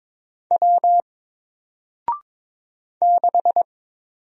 Morse code, heard as a keyed tone, about 22 wpm, 700 Hz. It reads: W 6
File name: Single-Letter-Number-22wpm